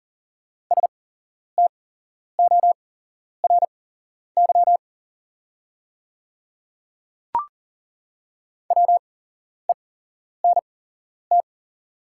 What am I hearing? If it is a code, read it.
Morse code, heard as a keyed tone, about 40 wpm, spaced as for 11 wpm, 690 Hz. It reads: STORY WENT
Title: 8x Character Spacing - Top 300 Words - 40wpm